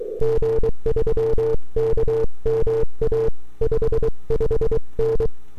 Morse code, heard as a keyed tone, about 22 wpm, 450 Hz. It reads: G3KMA 55N
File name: n6tr / 12Oct2009-160